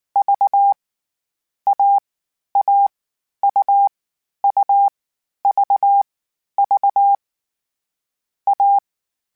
Morse code, heard as a keyed tone, about 19 wpm, 800 Hz. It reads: V A A U U V V A
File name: Difficult Combos / Difficult Combos-mp3s / AUV AUV4 @18